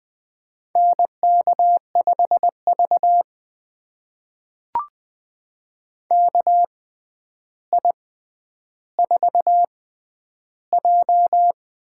Morse code, heard as a keyed tone, about 20 wpm, spaced as for 10 wpm, 700 Hz. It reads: NK5V KI4J
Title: USA Call Signs - 4 Characters - 6x Character Spacing - 20wpm